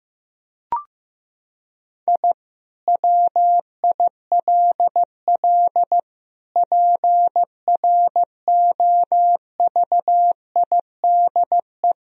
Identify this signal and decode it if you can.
Morse code, heard as a keyed tone, about 15 wpm, 700 Hz. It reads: I WILL PROVIDE
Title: Sentences from Top 1000 Words - Review Part 3 - 15wpm